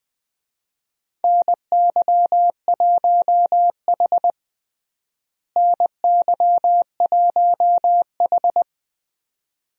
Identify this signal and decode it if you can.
Morse code, heard as a keyed tone, about 20 wpm, 700 Hz. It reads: NY1H NY1H
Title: USA Call Signs - 4 Characters - Repeated 2x - 20wpm